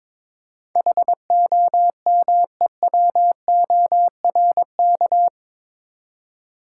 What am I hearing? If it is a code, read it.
Morse code, heard as a keyed tone, about 22 wpm, 700 Hz. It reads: HOMEWORK
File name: practice / 22wpm / Eight Letter Words - 22wpm